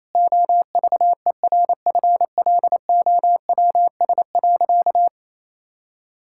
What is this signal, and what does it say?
Morse code, heard as a keyed tone, about 28 wpm, 700 Hz. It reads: OVERFLOWS.
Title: Pangram Challenge v2 - 28wpm